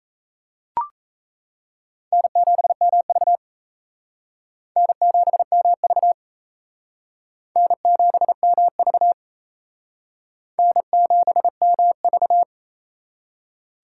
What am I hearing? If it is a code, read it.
Morse code, heard as a keyed tone, about 35 wpm, 700 Hz. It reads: N7MV N7MV N7MV N7MV